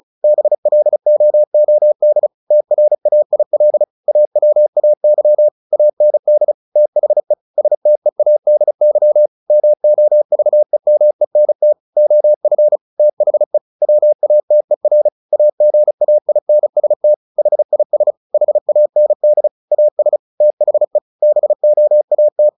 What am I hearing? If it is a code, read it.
Morse code, heard as a keyed tone, about 36 wpm, 600 Hz. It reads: BLOOD TRAIL AWAY AND THE STEADY MOVEMENT OF THE WATER AGAINST HIS HAND AS THE BOAT
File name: the old man and the sea / the old man and the sea 1 35wpm